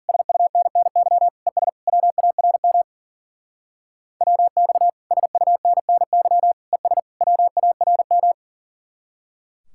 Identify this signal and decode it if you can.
Morse code, heard as a keyed tone, about 49 wpm, 700 Hz. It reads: SUNNY ES WARM WX SUNNY ES WARM
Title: Speed-Racing Common QSO Phrases - 40wpm